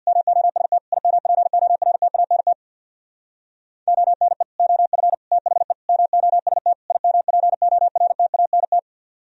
Morse code, heard as a keyed tone, about 60 wpm, 700 Hz. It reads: MOST IMPORTANT ONE OF THE MOST IMPORTANT
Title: Speed-Racing - Sets of 5 Words 50wpm